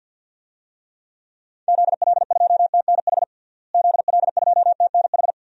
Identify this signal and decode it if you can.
Morse code, heard as a keyed tone, about 47 wpm, 700 Hz. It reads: ZL2TNH ZL2TNH
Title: DXCC-100 Call Signs - 6 Characters - Repeated 2x - 50wpm